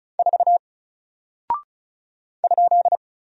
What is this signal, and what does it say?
Morse code, heard as a keyed tone, about 35 wpm, 700 Hz. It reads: 4 ?
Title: Single Character (Letter:Number:Punctuation:Prosign) 35wpm